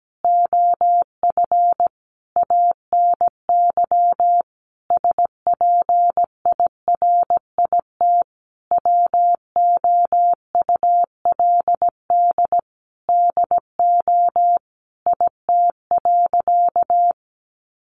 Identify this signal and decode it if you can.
Morse code, heard as a keyed tone, about 17 wpm, 700 Hz. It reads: OF ANY SPIRIT WOULD DO IT.
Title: Sentences from Top 1400 Words 17wpm